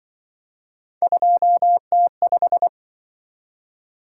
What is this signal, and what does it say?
Morse code, heard as a keyed tone, about 24 wpm, 700 Hz. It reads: 2T5
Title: Speed-Racing - Three Letter-Number Combinations - 20wpm